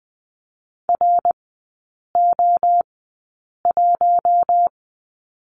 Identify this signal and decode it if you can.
Morse code, heard as a keyed tone, about 20 wpm, spaced as for 8 wpm, 700 Hz. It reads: RO1
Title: 178 Lesson 29e - Slash - Three Characters 20wpm